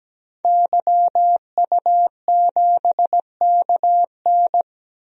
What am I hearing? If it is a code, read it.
Morse code, heard as a keyed tone, about 17 wpm, 700 Hz. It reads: YU7KN
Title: DXCC-100 Call Signs - 5 Characters - 17wpm